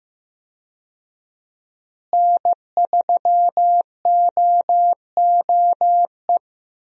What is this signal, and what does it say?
Morse code, heard as a keyed tone, about 15 wpm, 700 Hz. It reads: N3OOE
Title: USA Call Signs - 5 Characters - Repeated 2x - 15wpm